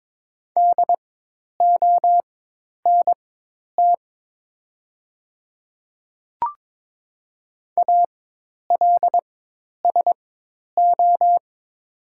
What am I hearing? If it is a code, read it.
Morse code, heard as a keyed tone, about 22 wpm, spaced as for 10 wpm, 700 Hz. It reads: DONT ALSO
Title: 4x Character Spacing - Top 500 Words - 22wpm